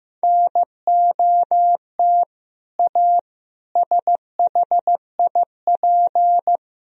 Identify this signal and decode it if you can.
Morse code, heard as a keyed tone, about 15 wpm, 700 Hz. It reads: NOT A SHIP
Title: Sentences from Top 1000 Words - Review Part 2 - 15wpm